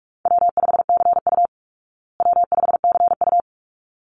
Morse code, heard as a keyed tone, about 44 wpm, 700 Hz. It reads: W5CU W5CU
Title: Call Signs Repeated 3x - 45wpm